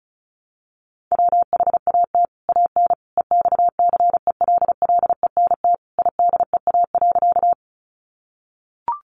Morse code, heard as a keyed tone, about 34 wpm, 700 Hz. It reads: WHAT AN EXCELLENT IDEA.
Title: Sentences from Top 1000 Words 35wpm